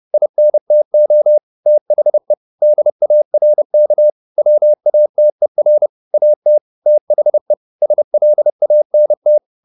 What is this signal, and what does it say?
Morse code, heard as a keyed tone, about 30 wpm, 600 Hz. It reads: INTO THE DARK WATER AT THE SLANT
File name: the old man and the sea / the old man and the sea 1 30wpm